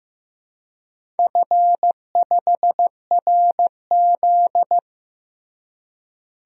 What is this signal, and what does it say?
Morse code, heard as a keyed tone, about 15 wpm, 700 Hz. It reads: F5RZ
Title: DXCC-100 Call Signs - 4 Characters - Repeated 3x - 15wpm